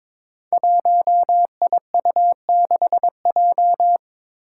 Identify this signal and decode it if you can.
Morse code, heard as a keyed tone, about 22 wpm, 700 Hz. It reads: 1IU6J
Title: Five Challenging Letter-Number Combinations 22wpm